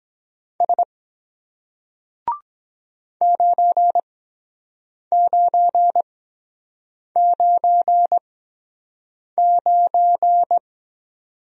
Morse code, heard as a keyed tone, about 23 wpm, spaced as for 7 wpm, 700 Hz. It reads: S 9999